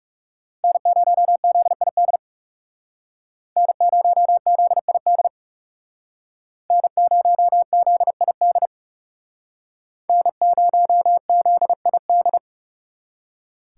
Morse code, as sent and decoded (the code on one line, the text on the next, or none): -. ----- --.. .. -.. / -. ----- --.. .. -.. / -. ----- --.. .. -.. / -. ----- --.. .. -..
N0ZID N0ZID N0ZID N0ZID